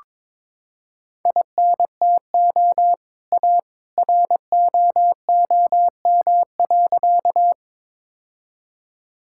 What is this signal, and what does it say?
Morse code, heard as a keyed tone, about 22 wpm, 700 Hz. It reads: INTO A ROOM.
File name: practice / 22wpm / Sentences with 2-Word Phrases - 22wpm